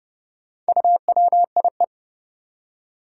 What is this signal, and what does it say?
Morse code, heard as a keyed tone, about 31 wpm, 700 Hz. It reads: UWIE
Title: Four Challenging Letter-Number Combinations 30wpm